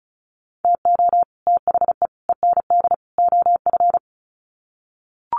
Morse code, heard as a keyed tone, about 34 wpm, 700 Hz. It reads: TO THE END OF
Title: Sets of 4 Words 35wpm